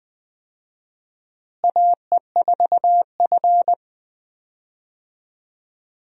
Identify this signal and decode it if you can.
Morse code, heard as a keyed tone, about 20 wpm, 700 Hz. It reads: AE4F
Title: USA Call Signs - 4 Characters - Repeated 3x - 20wpm